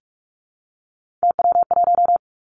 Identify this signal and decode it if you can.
Morse code, heard as a keyed tone, about 46 wpm, 700 Hz. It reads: TW1
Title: Three Challenging Letter-Number Combinations 45wpm